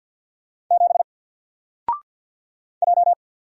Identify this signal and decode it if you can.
Morse code, heard as a keyed tone, about 52 wpm, 700 Hz. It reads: 7 J
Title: Single Letter-Number 50wpm